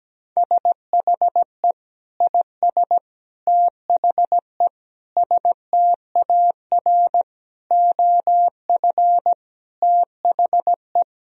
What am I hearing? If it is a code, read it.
Morse code, heard as a keyed tone, about 17 wpm, 700 Hz. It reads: SHE IS THE STAR OF THE
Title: Sentences from Top 1000 Words - Review Part 2 - 17wpm